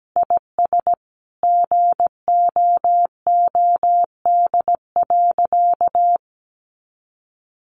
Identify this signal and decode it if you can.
Morse code, heard as a keyed tone, about 17 wpm, 700 Hz. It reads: IS GOOD.
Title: Sentences from Top 1200 Words 17wpm